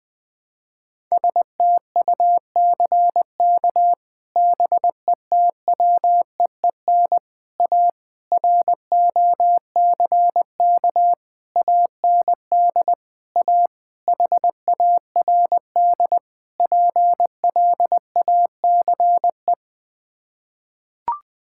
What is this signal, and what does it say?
Morse code, heard as a keyed tone, about 20 wpm, 700 Hz. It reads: STUCK BETWEEN A ROCK AND A HARD PLACE
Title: Sets of 8 Words - 20wpm